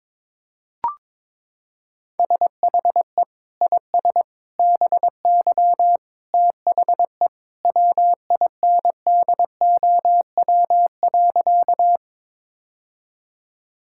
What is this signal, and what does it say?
Morse code, heard as a keyed tone, about 22 wpm, 700 Hz. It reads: SHE IS BY THE WINDOW.
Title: Sentences from Top 1600 Words 22wpm